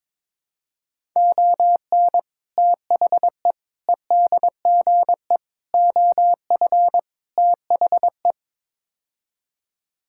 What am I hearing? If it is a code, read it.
Morse code, heard as a keyed tone, about 22 wpm, 700 Hz. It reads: ON THE EDGE OF THE
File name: practice / 22wpm / Build A Phrase - 5-Word Phrases - 22wpm